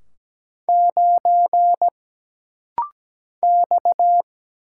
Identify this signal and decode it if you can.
Morse code, heard as a keyed tone, about 17 wpm, 700 Hz. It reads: 9 X
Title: Single Letter-Number - Rapid-Fire 17wpm